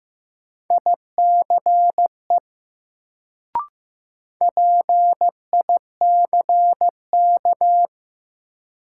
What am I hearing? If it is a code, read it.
Morse code, heard as a keyed tone, about 15 wpm, 700 Hz. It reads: ICE PICK
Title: Sentences from Top 1000 Words - New Word Drill - 15wpm